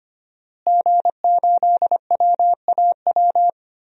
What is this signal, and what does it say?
Morse code, heard as a keyed tone, about 25 wpm, 700 Hz. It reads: G8WAW
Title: DXCC-100 Call Signs - 5 Characters - Repeated 3x - 25wpm